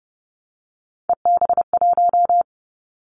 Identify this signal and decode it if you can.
Morse code, heard as a keyed tone, about 30 wpm, 700 Hz. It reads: EB1
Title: Callsign Prefixes for Top 25 DXCC - 30wpm